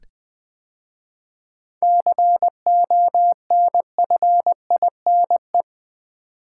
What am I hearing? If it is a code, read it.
Morse code, heard as a keyed tone, about 20 wpm, 700 Hz. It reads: CONFINE
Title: 7 Letter Words - First 3 Letters Given - 20wpm